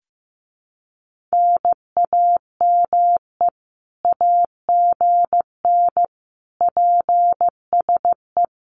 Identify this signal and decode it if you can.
Morse code, heard as a keyed tone, about 15 wpm, 700 Hz. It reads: NAME AGN PSE
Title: Common QSO Phrases Encore 15wpm